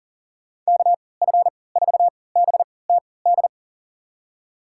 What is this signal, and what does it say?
Morse code, heard as a keyed tone, about 40 wpm, spaced as for 23 wpm, 700 Hz. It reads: KF4BTD
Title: USA Call Signs - 6 Characters - 3x Character Spacing - 40wpm